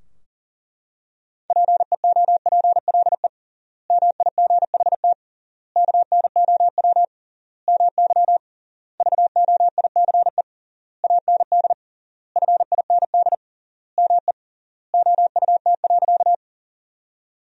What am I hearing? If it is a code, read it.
Morse code, heard as a keyed tone, about 37 wpm, 700 Hz. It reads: PEOPLE MIGHT KNOW MY VOICE AND FIND ME OUT.
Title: 3x Word-Spacing - Sentences from Top 500 Words - 40wpm